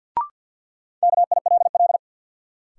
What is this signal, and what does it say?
Morse code, heard as a keyed tone, about 46 wpm, 700 Hz. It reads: KILL
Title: Sentences from Top 1400 Words - New Word Drill - 50wpm